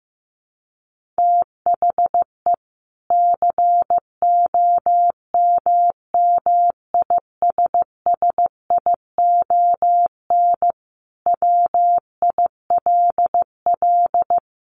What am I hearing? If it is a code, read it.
Morse code, heard as a keyed tone, about 15 wpm, 700 Hz. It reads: THE COMMISSION WILL
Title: Sentences from Top 1400 Words 15wpm